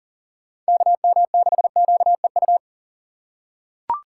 Morse code, heard as a keyed tone, about 40 wpm, 700 Hz. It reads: KM6QEU
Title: USA Call Signs - 2x Character Spacing - 40wpm